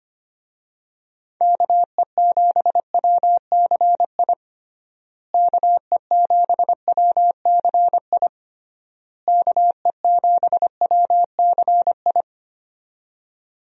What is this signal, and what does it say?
Morse code, heard as a keyed tone, about 25 wpm, 700 Hz. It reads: KE7WCI KE7WCI KE7WCI
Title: USA Call Signs - 6 Characters - Repeated 3x - 25wpm